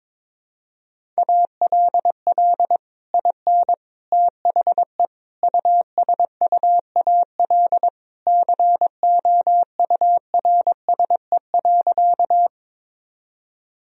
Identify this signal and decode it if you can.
Morse code, heard as a keyed tone, about 22 wpm, 700 Hz. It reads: ALL IN THE USUAL COURSE.
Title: Sentences from Top 1000 Words - Review Part 2 - 22wpm